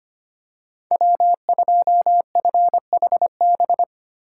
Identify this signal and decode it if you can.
Morse code, heard as a keyed tone, about 25 wpm, 700 Hz. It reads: W2FHB
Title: USA Call Signs - 5 Characters - 3x Character Spacing - 25wpm